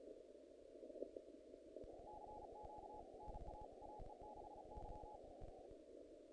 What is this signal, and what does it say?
Morse code, heard as a keyed tone, about 34 wpm, 840 Hz, with noise. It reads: CQ NA AE6Y
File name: contests / naqp-cw-k5tr-op-k5pi- -1800